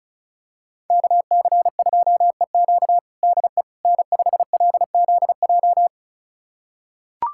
Morse code, heard as a keyed tone, about 34 wpm, 700 Hz. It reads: KC2EQ DE N5LZJ
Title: USA Callsign DE Callsign - 5 Characters - 35wpm